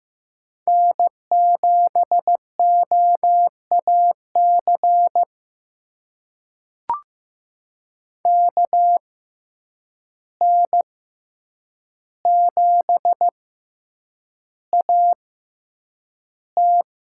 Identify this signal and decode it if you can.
Morse code, heard as a keyed tone, about 15 wpm, spaced as for 8 wpm, 700 Hz. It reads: N7OAC KN7AT
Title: USA Call Signs - 5 Characters - 6x Character Spacing - 15wpm